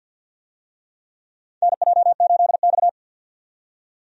Morse code, heard as a keyed tone, about 52 wpm, 700 Hz. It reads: NJ8X
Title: USA Call Signs - 4 Characters - 50wpm